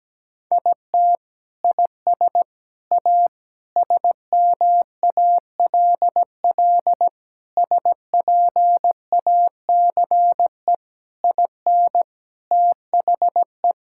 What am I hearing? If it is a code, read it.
Morse code, heard as a keyed tone, about 17 wpm, 700 Hz. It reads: IT IS A SMALL SPACE IN THE
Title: Sentences from Top 1000 Words - Review Part 5 - 17wpm